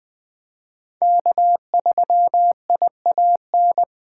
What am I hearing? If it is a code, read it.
Morse code, heard as a keyed tone, about 20 wpm, 700 Hz. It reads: K3IAN
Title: Speed Racing - USA 1x3 Call Signs 17wpm